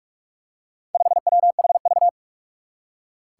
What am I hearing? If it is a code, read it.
Morse code, heard as a keyed tone, about 44 wpm, 700 Hz. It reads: 5WHV
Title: Four Challenging Letter-Number Combinations 45wpm